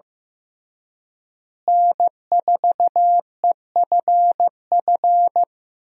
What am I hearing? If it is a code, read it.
Morse code, heard as a keyed tone, about 15 wpm, 700 Hz. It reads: N4EFF
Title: USA Call Signs - 5 Characters - Repeated 3x - 15wpm